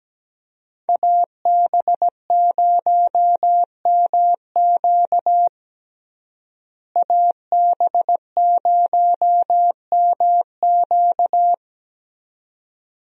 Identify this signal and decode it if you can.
Morse code, heard as a keyed tone, about 17 wpm, 700 Hz. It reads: AB0MQ AB0MQ
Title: USA Call Signs - 5 Characters - Repeated 2x - 17wpm